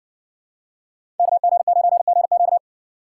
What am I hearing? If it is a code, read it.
Morse code, heard as a keyed tone, about 60 wpm, 700 Hz. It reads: KG9GQ